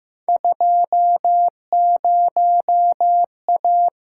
Morse code, heard as a keyed tone, about 15 wpm, 700 Hz. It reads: 20A
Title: Field Day Exchanges Encore - 15wpm